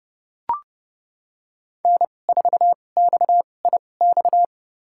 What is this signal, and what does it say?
Morse code, heard as a keyed tone, about 30 wpm, spaced as for 22 wpm, 700 Hz. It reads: N4XIX
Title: USA Call Signs - 5 Characters - 2x Character Spacing - 30wpm